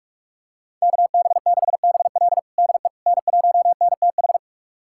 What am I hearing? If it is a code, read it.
Morse code, heard as a keyed tone, about 46 wpm, 700 Hz. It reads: KB6BL DE N1NTH
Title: USA Callsign DE Callsign - 5 Characters - 45wpm